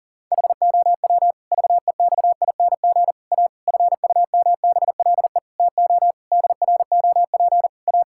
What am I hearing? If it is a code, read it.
Morse code, heard as a keyed tone, about 37 wpm, 700 Hz. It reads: HOW VEXING A FUMBLE TO DROP A
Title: Pangram Challenge v2 - 40wpm